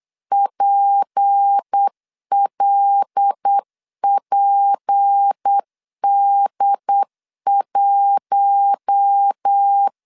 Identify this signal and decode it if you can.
Morse code, heard as a keyed tone, about 8 wpm, 800 Hz. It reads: PLPD1
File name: LEZIONI / 6 PKX0 / 6 P 8wpm 3025